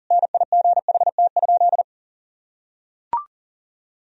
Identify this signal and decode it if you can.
Morse code, heard as a keyed tone, about 40 wpm, 700 Hz. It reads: NIGHT?
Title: Sentences from Top 1000 Words - Review Part 2 - 40wpm